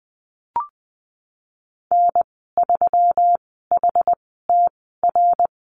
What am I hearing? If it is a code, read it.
Morse code, heard as a keyed tone, about 20 wpm, spaced as for 15 wpm, 700 Hz. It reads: N3HTR
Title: 128 Lesson 21g - Number 6 - Call Signs 20wpm